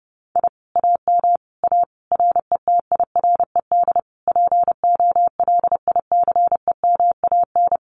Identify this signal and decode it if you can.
Morse code, heard as a keyed tone, about 30 wpm, 700 Hz. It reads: I AM A RETIRED POLICEMAN
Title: Common QSO Phrases Encore 30wpm